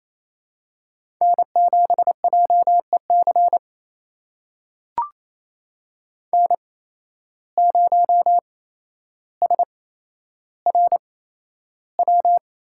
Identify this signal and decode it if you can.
Morse code, heard as a keyed tone, about 28 wpm, spaced as for 12 wpm, 700 Hz. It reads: N7JEC N0SRW
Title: USA Call Signs - 5 Characters - 8x Character Spacing - 28wpm